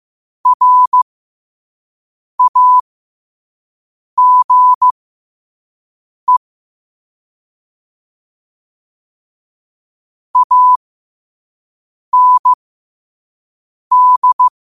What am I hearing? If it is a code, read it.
Morse code, heard as a keyed tone, about 15 wpm, spaced as for 5 wpm, 1000 Hz. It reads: RAGE AND
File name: qotdfiles / 5wpm